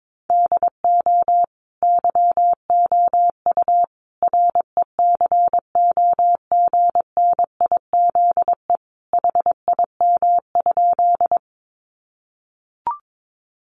Morse code, heard as a keyed tone, about 22 wpm, 700 Hz. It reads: DO YOU RECOGNIZE HIM?
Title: Sentences from Top 1400 Words 22wpm